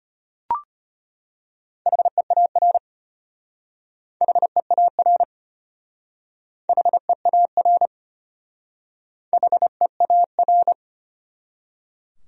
Morse code, heard as a keyed tone, about 32 wpm, 700 Hz. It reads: HEAR HEAR HEAR HEAR